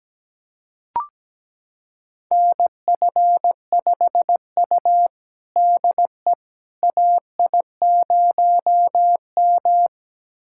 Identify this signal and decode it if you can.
Morse code, heard as a keyed tone, about 17 wpm, 700 Hz. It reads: NF5U DE AI0M
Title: USA Callsign DE Callsign - 4 Characters - 17wpm